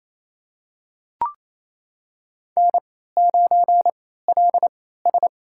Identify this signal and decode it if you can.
Morse code, heard as a keyed tone, about 29 wpm, spaced as for 16 wpm, 700 Hz. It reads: N9LS
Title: USA Call Signs - 4 Characters - 3x Character Spacing - 28wpm